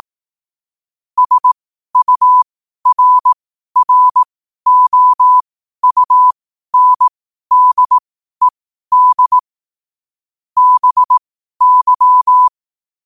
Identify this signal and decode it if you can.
Morse code, heard as a keyed tone, about 18 wpm, spaced as for 12 wpm, 1000 Hz. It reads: SURROUNDED BY
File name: qotdfiles / 12wpm